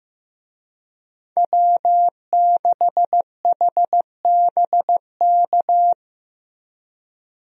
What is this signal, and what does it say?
Morse code, heard as a keyed tone, about 15 wpm, 700 Hz. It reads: W6HBK
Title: USA Call Signs - 5 Characters - Repeated 3x - 15wpm